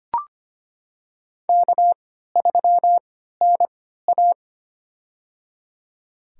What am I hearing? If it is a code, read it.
Morse code, heard as a keyed tone, about 25 wpm, spaced as for 14 wpm, 700 Hz. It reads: K3NA
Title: USA Call Signs - 4 Characters - 3x Character Spacing - 25wpm